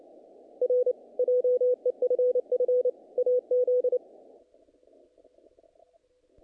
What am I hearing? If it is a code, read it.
Morse code, heard as a keyed tone, about 29 wpm, 500 Hz. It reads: R JEFF AZ
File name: contests / naqp-cw-k5tr-op-k5pi- -1800